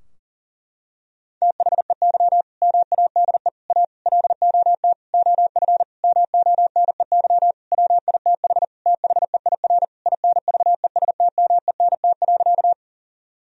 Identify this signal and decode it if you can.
Morse code, heard as a keyed tone, about 40 wpm, 700 Hz. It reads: THEY MADE A LOT OF MONEY WITH THEIR INVESTMENT.
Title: Sentences from Top 1000 Words - Review Part 4 - 40wpm